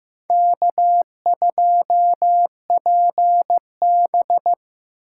Morse code, heard as a keyed tone, about 18 wpm, 700 Hz. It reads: K2PB
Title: Speed Racing - USA Call Signs 15wpm